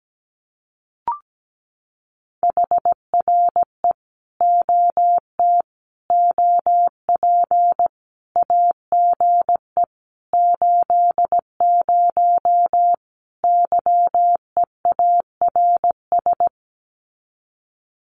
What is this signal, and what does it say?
Morse code, heard as a keyed tone, about 17 wpm, 700 Hz. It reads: HRE OT OP AGE 80 YEARS
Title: Common QSO Phrases 17wpm